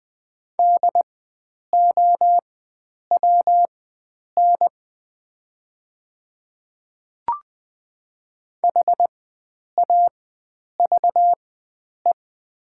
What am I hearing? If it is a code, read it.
Morse code, heard as a keyed tone, about 20 wpm, spaced as for 9 wpm, 700 Hz. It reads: DOWN HAVE
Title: 4x Character Spacing - Top 100 Words - 20wpm